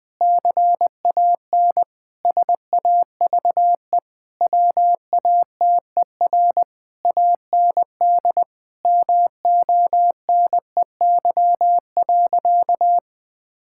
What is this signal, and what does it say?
Morse code, heard as a keyed tone, about 20 wpm, 700 Hz. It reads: CAN SAVE WATER AND MONEY.
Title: Difficult Sentences with Hints - 20wpm